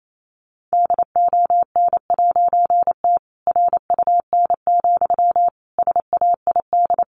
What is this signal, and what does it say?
Morse code, heard as a keyed tone, about 28 wpm, 700 Hz. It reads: DON'T RUN, SAID
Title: A Sound Of Thunder 28wpm